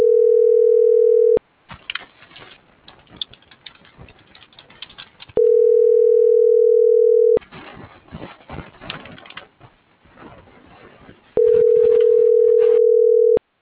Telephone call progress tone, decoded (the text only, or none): ringback tone